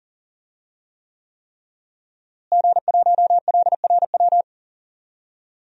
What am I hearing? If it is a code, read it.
Morse code, heard as a keyed tone, about 40 wpm, 700 Hz. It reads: G1LRW
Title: DXCC-100 Call Signs - 5 Characters - Repeated 3x - 40wpm